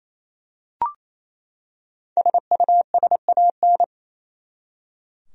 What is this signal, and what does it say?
Morse code, heard as a keyed tone, about 28 wpm, 700 Hz. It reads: SUSAN
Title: USA Common Names - Max 5 Letters - 28wpm